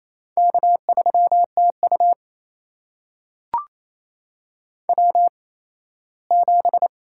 Morse code, heard as a keyed tone, about 28 wpm, spaced as for 17 wpm, 700 Hz. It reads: K3TU W7
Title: USA Call Signs - 4 Characters - 8x Character Spacing - 28wpm